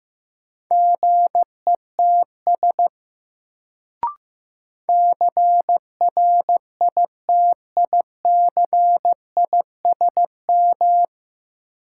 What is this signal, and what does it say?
Morse code, heard as a keyed tone, about 15 wpm, 700 Hz. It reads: GETS CRITICISM
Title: Sentences from Top 1300 Words - New Word Drill - 15wpm